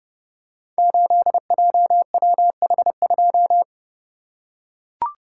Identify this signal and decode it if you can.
Morse code, heard as a keyed tone, about 30 wpm, 700 Hz. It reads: 8JWH2
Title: Code Groups - 5 Letter:Numbers - 30wpm